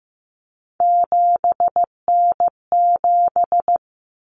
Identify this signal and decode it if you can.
Morse code, heard as a keyed tone, about 15 wpm, 700 Hz. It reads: 7N7
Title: Common Cut Numbers - 3-Digit 15wpm